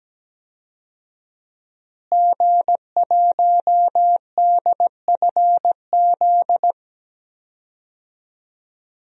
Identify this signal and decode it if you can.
Morse code, heard as a keyed tone, about 17 wpm, 700 Hz. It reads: G1DFZ
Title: DXCC-100 Call Signs - 5 Characters - Repeated 3x - 17wpm